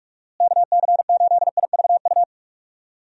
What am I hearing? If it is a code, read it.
Morse code, heard as a keyed tone, about 46 wpm, 700 Hz. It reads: KC8IVU
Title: USA Call Signs - 6 Characters - Repeated 3x - 45wpm